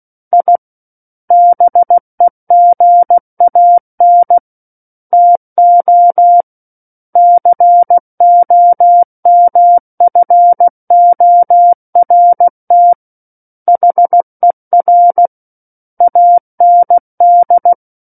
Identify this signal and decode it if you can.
Morse code, heard as a keyed tone, about 16 wpm, 700 Hz. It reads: I BEGAN TO COMFORT HER AND
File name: Book One / 07-Chapter 7 - 16 WPM